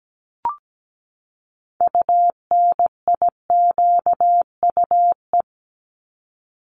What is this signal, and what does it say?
Morse code, heard as a keyed tone, about 17 wpm, 700 Hz. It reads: UNIQUE
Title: Top 751-1000 Words 17wpm